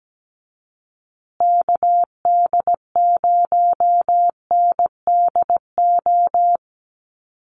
Morse code, heard as a keyed tone, about 17 wpm, 700 Hz. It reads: KD0NDO
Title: USA Call Signs Repeated 3x 17wpm